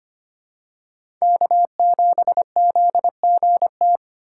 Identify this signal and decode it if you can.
Morse code, heard as a keyed tone, about 25 wpm, 700 Hz. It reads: K7ZGT
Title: USA Call Signs - 5 Characters - 4x Character Spacing - 25wpm